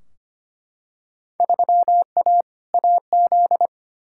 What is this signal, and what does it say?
Morse code, heard as a keyed tone, about 25 wpm, 700 Hz. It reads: 3A AZ
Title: Field Day Exchanges - 25wpm